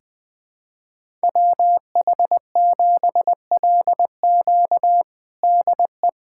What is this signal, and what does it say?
Morse code, heard as a keyed tone, about 20 wpm, 700 Hz. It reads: WH7LQ DE
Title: USA Callsign DE Callsign - 5 Characters - 20wpm